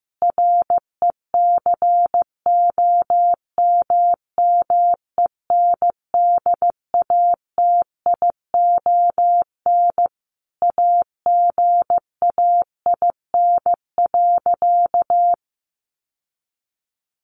Morse code, heard as a keyed tone, about 15 wpm, 700 Hz. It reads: RECOMMENDATION AGAIN.
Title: Sentences from Top 1400 Words 15wpm